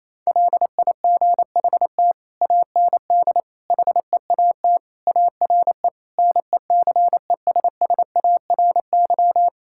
Morse code, heard as a keyed tone, about 28 wpm, 700 Hz. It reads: LIGHT AND HEAT ARE NECESSARY
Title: Sentences from Top 1000 Words - Review Part 5 - 28wpm